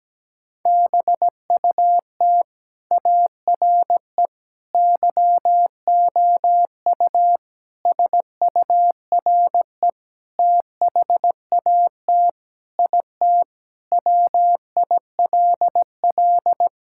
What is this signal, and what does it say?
Morse code, heard as a keyed tone, about 17 wpm, 700 Hz. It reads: BUT ARE YOU SURE THAT IT WILL
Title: Sentences from Top 1600 Words 17wpm